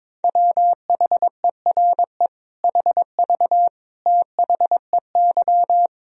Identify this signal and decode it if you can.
Morse code, heard as a keyed tone, about 22 wpm, 700 Hz. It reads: WHERE HV THEY
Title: Sentences from Top 200 Words and Abbreviations 22wpm